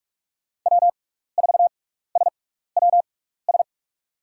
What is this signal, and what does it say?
Morse code, heard as a keyed tone, about 46 wpm, spaced as for 16 wpm, 700 Hz. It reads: W4SWS
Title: USA Call Signs - 5 Characters - 6x Character Spacing - 45wpm